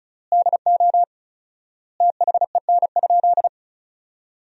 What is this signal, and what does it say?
Morse code, heard as a keyed tone, about 36 wpm, 700 Hz. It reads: DO THEN?
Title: 4x Word-Spacing - Sentences from Top 100 Words - 35wpm